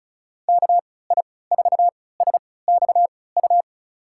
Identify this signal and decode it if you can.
Morse code, heard as a keyed tone, about 36 wpm, spaced as for 20 wpm, 700 Hz. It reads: KI4SXU
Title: USA Call Signs - 6 Characters - 3x Character Spacing - 35wpm